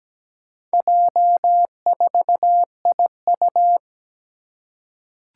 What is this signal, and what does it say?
Morse code, heard as a keyed tone, about 17 wpm, 700 Hz. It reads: J4IU
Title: Four Challenging Letter-Number Combinations 17wpm